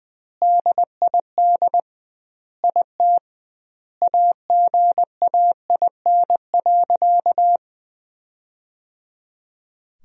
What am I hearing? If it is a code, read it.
Morse code, heard as a keyed tone, about 20 wpm, 700 Hz. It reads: DID IT AGAIN.
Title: 2x Word-Spacing - Sentences from Top 300 Words - 20wpm